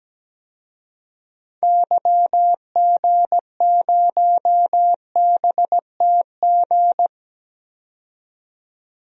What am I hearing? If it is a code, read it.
Morse code, heard as a keyed tone, about 17 wpm, 700 Hz. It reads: YG0BTG
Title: DXCC-100 Call Signs - 6 Characters - 17wpm